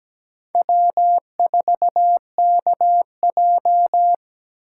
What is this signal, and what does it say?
Morse code, heard as a keyed tone, about 20 wpm, 700 Hz. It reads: W4KJ
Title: Speed Racing - USA 1x2 Call Signs 17wpm